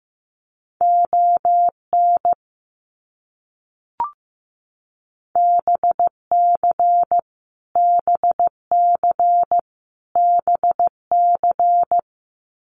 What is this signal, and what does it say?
Morse code, heard as a keyed tone, about 15 wpm, 700 Hz. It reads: ON BC BC BC
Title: Canadian Provinces and Territories - Repeated 3x - 15wpm